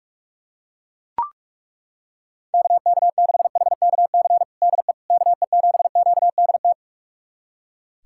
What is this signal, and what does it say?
Morse code, heard as a keyed tone, about 41 wpm, 700 Hz. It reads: KK6HKC DE KE7QDT
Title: USA Callsign DE Callsign - 6 Characters - 45wpm